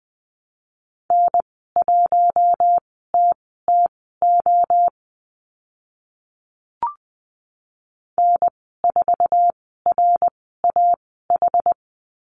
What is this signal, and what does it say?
Morse code, heard as a keyed tone, about 20 wpm, spaced as for 14 wpm, 700 Hz. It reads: N1TTO N4RAH
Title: 062 Lesson 11g - Letter h - Call Signs 20wpm